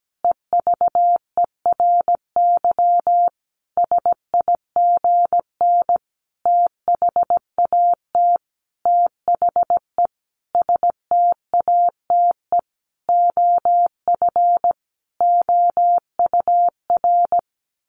Morse code, heard as a keyed tone, about 17 wpm, 700 Hz. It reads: EVERY SIGN THAT THE STATE OF OUR
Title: Sentences from Top 1400 Words 17wpm